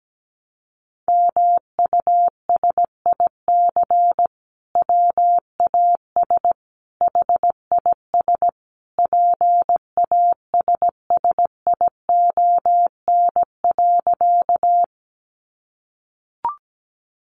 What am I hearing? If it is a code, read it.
Morse code, heard as a keyed tone, about 17 wpm, 700 Hz. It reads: MUSIC WAS HIS PASSION.
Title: Sentences from Top 1300 Words 17wpm